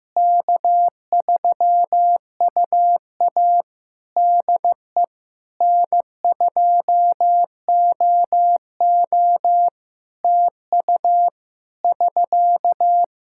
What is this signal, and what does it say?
Morse code, heard as a keyed tone, about 16 wpm, 700 Hz. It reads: K3UA DE N2OO TU <SK>
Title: K1USN SST QSO Practice 15wpm